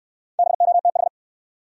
Morse code, heard as a keyed tone, about 62 wpm, 700 Hz. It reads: BOTH